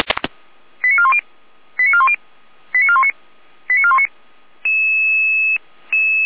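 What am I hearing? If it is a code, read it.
Five-tone selective call (ZVEI), heard as quick runs of five tones, tones of about 70 ms. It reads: 89410 89410 89419 89419
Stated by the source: Received by SDR, ZVEI five-tone call.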